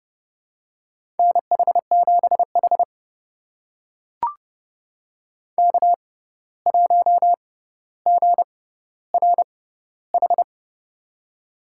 Morse code, heard as a keyed tone, about 30 wpm, spaced as for 14 wpm, 700 Hz. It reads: NH7H K1GRH
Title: USA Call Signs - 6x Character Spacing - 30wpm